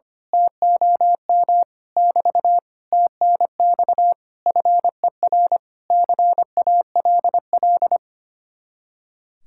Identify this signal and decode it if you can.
Morse code, heard as a keyed tone, about 24 wpm, 700 Hz. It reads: TOM <BT> TNX FER CALL
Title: Speed-Racing Common QSO Phrases Encore - 25wpm